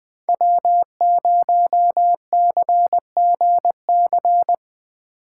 Morse code, heard as a keyed tone, about 20 wpm, 700 Hz. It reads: W0CGC
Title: 2023 POTA Hunter Call Signs - 20wpm